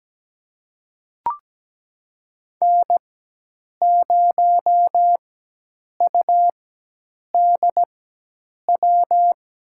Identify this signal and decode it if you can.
Morse code, heard as a keyed tone, about 17 wpm, spaced as for 8 wpm, 700 Hz. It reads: N0UDW
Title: USA Call Signs - 5 Characters - 4x Character Spacing - 17wpm